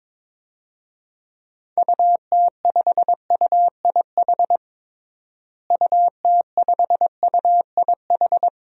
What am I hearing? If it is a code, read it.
Morse code, heard as a keyed tone, about 22 wpm, 700 Hz. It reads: UT5UIH UT5UIH
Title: DXCC-100 Call Signs - 6 Characters - Repeated 2x - 22wpm